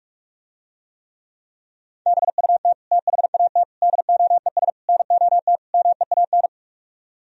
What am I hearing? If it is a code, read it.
Morse code, heard as a keyed tone, about 45 wpm, 700 Hz. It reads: BUT THAT DOES NOT MEAN